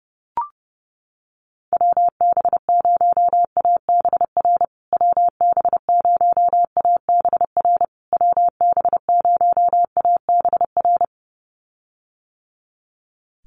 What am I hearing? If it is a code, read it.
Morse code, heard as a keyed tone, about 30 wpm, 700 Hz. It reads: WB0ABR WB0ABR WB0ABR
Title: USA Call Signs Repeated 3x 30wpm